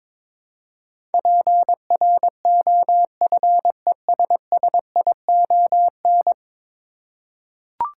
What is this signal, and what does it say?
Morse code, heard as a keyed tone, about 22 wpm, 700 Hz. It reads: PROFESSION
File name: practice / 22wpm / 10 Letter Words - First 3 Letters Given - 22wpm